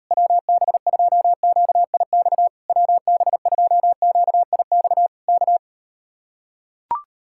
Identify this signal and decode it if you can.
Morse code, heard as a keyed tone, about 38 wpm, 700 Hz. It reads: WB2QIX WB2QIX K